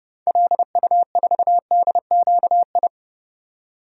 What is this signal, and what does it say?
Morse code, heard as a keyed tone, about 30 wpm, 700 Hz. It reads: LU4DQI
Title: DXCC-100 Call Signs - 6 Characters - 30wpm